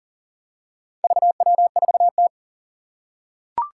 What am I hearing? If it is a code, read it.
Morse code, heard as a keyed tone, about 40 wpm, 700 Hz. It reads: VW4T
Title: Four Challenging Letter-Number Combinations 40wpm